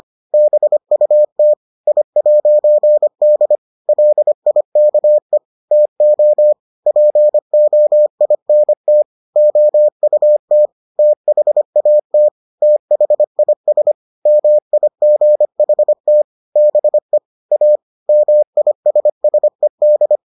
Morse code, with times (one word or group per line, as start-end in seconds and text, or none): 0.3-1.5 BUT
1.9-3.6 I'D
3.9-5.4 LIKE
5.7-6.5 TO
6.9-9.0 POINT
9.4-10.7 OUT
11.0-12.3 THAT
12.6-13.9 THIS
14.3-16.2 MIGHT
16.6-17.2 BE
17.5-17.8 A
18.1-20.2 MISSED